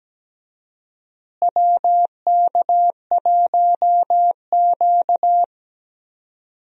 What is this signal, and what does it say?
Morse code, heard as a keyed tone, about 17 wpm, 700 Hz. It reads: WK1Q
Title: USA Call Signs - 4 Characters - Repeated 2x - 17wpm